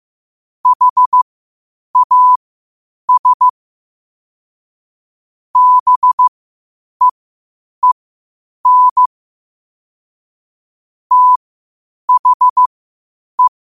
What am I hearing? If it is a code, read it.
Morse code, heard as a keyed tone, about 15 wpm, spaced as for 8 wpm, 1000 Hz. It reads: HAS BEEN THE
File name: qotdfiles / 8wpm